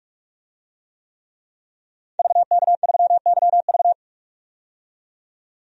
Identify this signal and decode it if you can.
Morse code, heard as a keyed tone, about 46 wpm, 700 Hz. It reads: VK3YV
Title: DXCC-100 Call Signs - 5 Characters - 45wpm